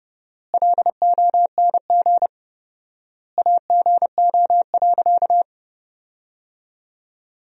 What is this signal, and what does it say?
Morse code, heard as a keyed tone, about 30 wpm, 700 Hz. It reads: LONG AGO.
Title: 4x Word-Spacing - Sentences from Top 500 Words - 30wpm